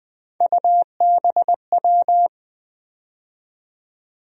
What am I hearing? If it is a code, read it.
Morse code, heard as a keyed tone, about 20 wpm, 700 Hz. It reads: UBW
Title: USA Call Sign Postfixes - 3 Letters 20wpm